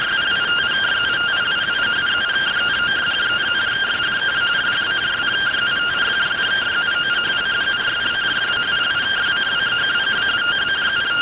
Two-tone teleprinter signal, about 170 Hz shift.